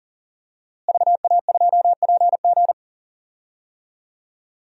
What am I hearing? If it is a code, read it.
Morse code, heard as a keyed tone, about 40 wpm, 700 Hz. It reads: VA2PG
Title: DXCC-100 Call Signs - 5 Characters - Repeated 3x - 40wpm